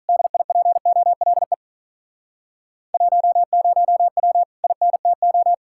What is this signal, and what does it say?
Morse code, heard as a keyed tone, about 45 wpm, 700 Hz. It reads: DIPOLE 10W INTO